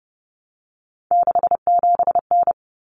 Morse code, heard as a keyed tone, about 30 wpm, 700 Hz. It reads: 67N
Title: Common Cut Numbers - 3-Digit 30wpm